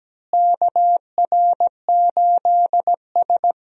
Morse code, heard as a keyed tone, about 17 wpm, 700 Hz. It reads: KR8S
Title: USA Call Signs - 4 Characters - Repeated 3x - 17wpm